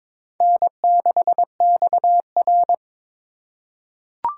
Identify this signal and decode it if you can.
Morse code, heard as a keyed tone, about 22 wpm, 700 Hz. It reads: N6XR
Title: USA Call Signs - 2x Character Spacing - 22wpm